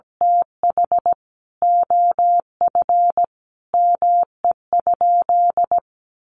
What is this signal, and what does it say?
Morse code, heard as a keyed tone, about 17 wpm, 700 Hz. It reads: TH OF ME?
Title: Sentences from Top 1100 Words 17wpm